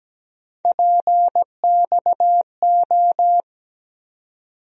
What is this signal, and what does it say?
Morse code, heard as a keyed tone, about 17 wpm, 700 Hz. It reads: PXO
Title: USA Call Sign Postfixes - 3 Letters 17wpm